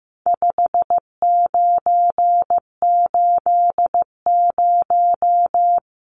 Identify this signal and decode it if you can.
Morse code, heard as a keyed tone, about 15 wpm, 700 Hz. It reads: 5980
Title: Numbers - 4-Digit 15wpm